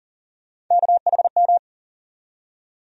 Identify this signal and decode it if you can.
Morse code, heard as a keyed tone, about 40 wpm, 700 Hz. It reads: KHM
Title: USA Call Sign Postfixes - 3 Letters 40wpm